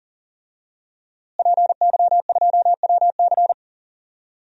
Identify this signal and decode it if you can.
Morse code, heard as a keyed tone, about 40 wpm, 700 Hz. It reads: PY2WC
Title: DXCC-100 Call Signs - 5 Characters - Repeated 2x - 40wpm